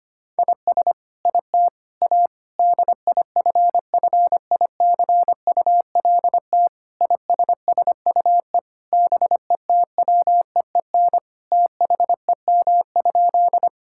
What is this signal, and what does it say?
Morse code, heard as a keyed tone, about 25 wpm, 700 Hz. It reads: IS IT A DIFFICULT ISSUE BETWEEN THEM?
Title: Sentences from Top 1000 Words - Review Part 6 - 25wpm